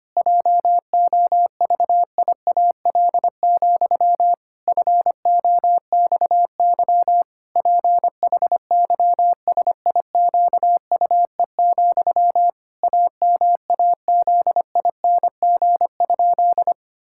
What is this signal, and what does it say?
Morse code, heard as a keyed tone, about 25 wpm, 700 Hz. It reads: JOVIAL, FOXY PHYSIQUE, AMAZING?
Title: Pangram Challenge v2 - 25wpm